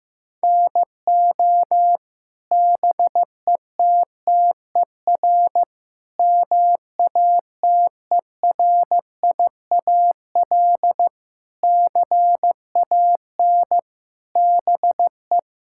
Morse code, heard as a keyed tone, about 14 wpm, 700 Hz. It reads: NO BETTER MATERIAL CAN BE
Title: Sentences from Top 1000 Words - Review Part 5 - 15wpm